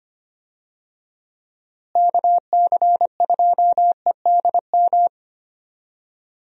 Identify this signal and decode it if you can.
Morse code, heard as a keyed tone, about 25 wpm, 700 Hz. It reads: KC2EDM
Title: USA Call Signs - 6 Characters - Repeated 2x - 25wpm